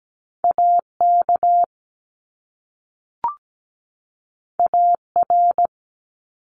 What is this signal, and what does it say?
Morse code, heard as a keyed tone, about 17 wpm, 700 Hz. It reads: AK AR
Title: US States 17wpm